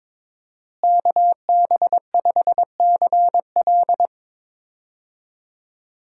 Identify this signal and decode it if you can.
Morse code, heard as a keyed tone, about 22 wpm, 700 Hz. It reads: KB5CL
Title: USA Call Signs - 5 Characters - Repeated 3x - 22wpm